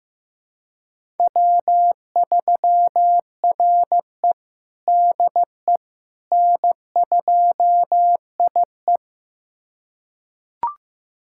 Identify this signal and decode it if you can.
Morse code, heard as a keyed tone, about 15 wpm, 700 Hz. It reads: W3RE DE N2IE
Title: USA Callsign DE Callsign - 4 Characters - 15wpm